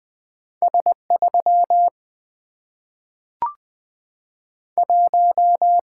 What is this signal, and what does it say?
Morse code, heard as a keyed tone, about 20 wpm, 700 Hz. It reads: S 3 1
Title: Single-Letter Single-Number 20wpm